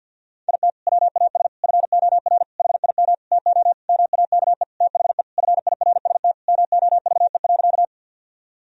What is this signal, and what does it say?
Morse code, heard as a keyed tone, about 53 wpm, 700 Hz. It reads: IT WAS FOR HIM TO MAKE THE FIRST MOVE.
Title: Sentences from Top 1000 Words - Review Part 1 - 50wpm